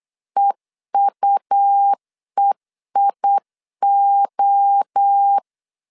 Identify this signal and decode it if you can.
Morse code, heard as a keyed tone, about 8 wpm, 800 Hz. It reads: EUEIO